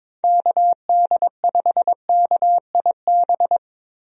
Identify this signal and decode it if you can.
Morse code, heard as a keyed tone, about 22 wpm, 700 Hz. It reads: KD5KIB
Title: USA Call Signs - 6 Characters - 22wpm